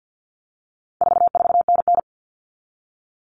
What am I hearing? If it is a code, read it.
Morse code, heard as a keyed tone, about 50 wpm, 700 Hz. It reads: 44NN
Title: Common Cut Numbers - 4-Digit 50wpm